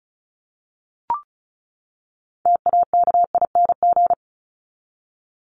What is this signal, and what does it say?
Morse code, heard as a keyed tone, about 34 wpm, 700 Hz. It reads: TAKING
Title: ING Suffix - Top 100 Words with Base Words 35wpm